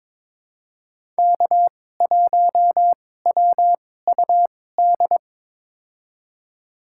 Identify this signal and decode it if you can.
Morse code, heard as a keyed tone, about 22 wpm, spaced as for 16 wpm, 700 Hz. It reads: K1WUD
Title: USA Call Signs - 5 Characters - 2x Character Spacing - 22wpm